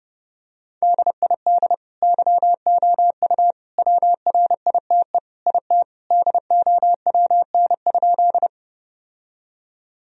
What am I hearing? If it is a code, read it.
Morse code, heard as a keyed tone, about 30 wpm, 700 Hz. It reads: DID YOU WRITE IT DOWN?
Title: Sentences from Top 1000 Words - Review Part 1 - 30wpm